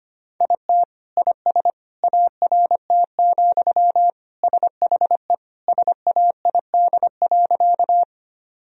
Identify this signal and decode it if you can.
Morse code, heard as a keyed tone, about 25 wpm, 700 Hz. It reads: IT IS ART, SHE SAID.
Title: Sentences from Top 1000 Words - Review Part 3 - 25wpm